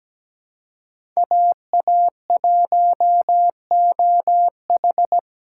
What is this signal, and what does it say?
Morse code, heard as a keyed tone, about 17 wpm, 700 Hz. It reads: AA1OH
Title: USA Call Signs - 2x Character Spacing - 17wpm